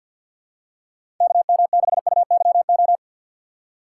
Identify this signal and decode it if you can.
Morse code, heard as a keyed tone, about 46 wpm, 700 Hz. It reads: KM6UYO
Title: USA Call Signs - 6 Characters - 4x Character Spacing - 50wpm